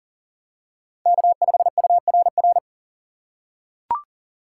Morse code, heard as a keyed tone, about 40 wpm, 700 Hz. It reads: K5URR
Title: USA Call Signs - 3x Character Spacing - 40wpm